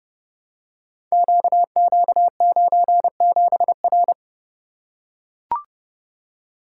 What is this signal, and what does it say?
Morse code, heard as a keyed tone, about 29 wpm, 700 Hz. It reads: QQ97R
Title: Code Groups - 5 Letter:Numbers - 30wpm